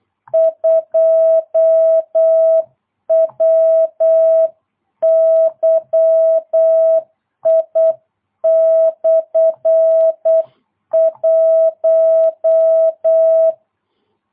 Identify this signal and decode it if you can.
Morse code, heard as a keyed tone, about 8 wpm, 650 Hz. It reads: 2WYI/1